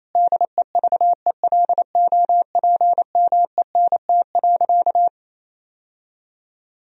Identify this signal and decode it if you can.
Morse code, heard as a keyed tone, about 28 wpm, 700 Hz. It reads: DEVELOPMENT.
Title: Sentences from Top 1000 Words - Review Part 4 - 28wpm